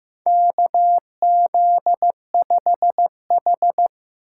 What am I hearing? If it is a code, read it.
Morse code, heard as a keyed tone, about 15 wpm, 700 Hz. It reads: KZ5H
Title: USA Call Signs - 4 Characters - 6x Character Spacing - 15wpm